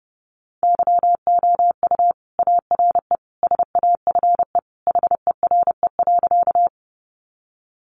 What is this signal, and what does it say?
Morse code, heard as a keyed tone, about 30 wpm, 700 Hz. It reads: YOU ARE SAFE HERE.
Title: Sentences from Top 900 Words 30wpm